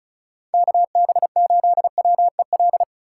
Morse code, heard as a keyed tone, about 36 wpm, 700 Hz. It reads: KB8WEL
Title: USA Call Signs - 2x Character Spacing - 35wpm